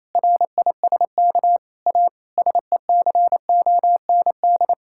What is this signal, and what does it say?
Morse code, heard as a keyed tone, about 28 wpm, 700 Hz. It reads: RISK A SECOND
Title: Sentences from Top 1600 Words 28wpm